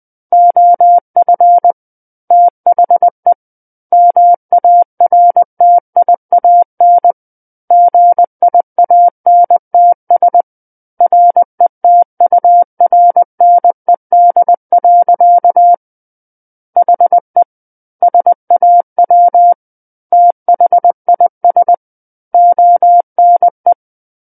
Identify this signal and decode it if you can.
Morse code, heard as a keyed tone, about 20 wpm, 700 Hz. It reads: OF THE MARTIAN GIANTS RETURNED. HE SAW THIS ONE
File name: 20 WPM